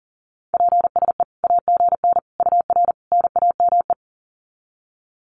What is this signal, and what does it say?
Morse code, heard as a keyed tone, about 40 wpm, 700 Hz. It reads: PSE AGN UR NAME
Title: Common QSO Phrases 40wpm